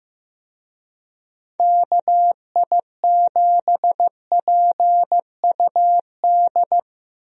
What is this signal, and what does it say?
Morse code, heard as a keyed tone, about 15 wpm, 700 Hz. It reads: KI7PUD
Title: USA Call Signs - 6 Characters - Repeated 2x - 15wpm